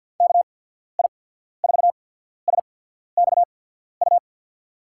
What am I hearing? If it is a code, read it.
Morse code, heard as a keyed tone, about 50 wpm, spaced as for 14 wpm, 700 Hz. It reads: KI4SXU
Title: USA Call Signs - 6 Characters - 8x Character Spacing - 50wpm